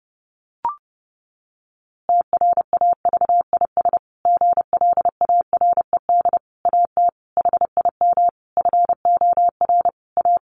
Morse code, heard as a keyed tone, about 30 wpm, 700 Hz. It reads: TRAVIS GLARED AT HIM FOR A
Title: A Sound Of Thunder 30wpm